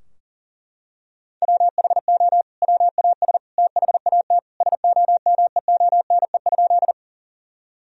Morse code, heard as a keyed tone, about 37 wpm, 700 Hz. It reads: WHO WAS THAT SOMEONE?
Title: Sentences from Top 1000 Words - Review Part 4 - 40wpm